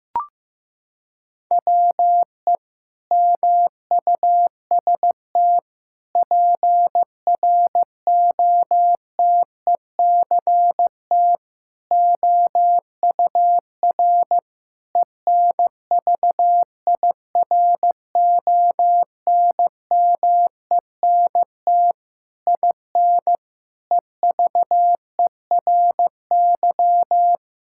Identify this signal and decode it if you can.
Morse code, heard as a keyed tone, about 15 wpm, 700 Hz. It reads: WE MUST PROTECT OUR ENVIRONMENT IN EVERY
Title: Sentences from Top 1000 Words - Review Part 5 - 15wpm